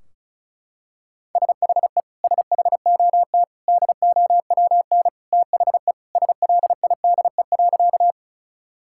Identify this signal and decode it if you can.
Morse code, heard as a keyed tone, about 36 wpm, 700 Hz. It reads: SHE SHOT DOWN THE SLIDE.
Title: Sentences from Top 1600 Words 35wpm